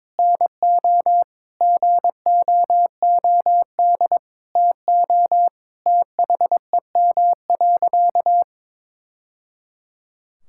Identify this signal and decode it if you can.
Morse code, heard as a keyed tone, about 22 wpm, 700 Hz. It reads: NO GOOD TO THEM.
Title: Sentences from Top 1000 Words - Review Part 6 - 22wpm